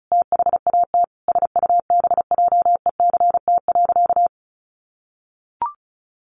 Morse code, heard as a keyed tone, about 34 wpm, 700 Hz. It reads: THAT SUBJECT.
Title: Sentences from Top 800 Words 35wpm